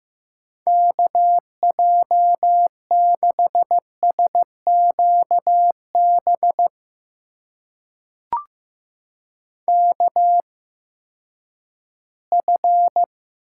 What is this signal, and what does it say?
Morse code, heard as a keyed tone, about 15 wpm, spaced as for 10 wpm, 700 Hz. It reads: KJ6SQB KF
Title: USA Call Signs - 8x Character Spacing - 15wpm